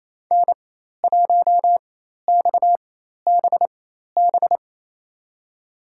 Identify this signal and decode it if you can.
Morse code, heard as a keyed tone, about 28 wpm, spaced as for 13 wpm, 700 Hz. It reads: N1XBB
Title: USA Call Signs - 4x Character Spacing - 28wpm